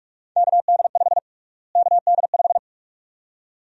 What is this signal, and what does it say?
Morse code, heard as a keyed tone, about 47 wpm, 700 Hz. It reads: KD5 KD5
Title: USA Call Sign Prefixes - Repeated 2x - 45wpm